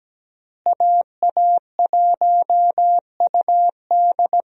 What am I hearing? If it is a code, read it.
Morse code, heard as a keyed tone, about 17 wpm, 700 Hz. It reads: AA1UD
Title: USA Call Signs - 5 Characters - Repeated 2x - 17wpm